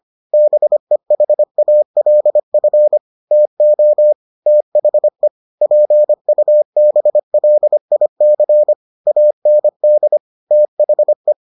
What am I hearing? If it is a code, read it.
Morse code, heard as a keyed tone, about 25 wpm, 600 Hz. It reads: BEHALF TO THE PUBLIC AND THE